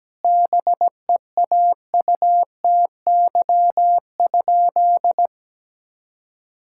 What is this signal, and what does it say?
Morse code, heard as a keyed tone, about 17 wpm, 700 Hz. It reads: BEAUTY?
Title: Sentences from Top 1000 Words - Review Part 5 - 17wpm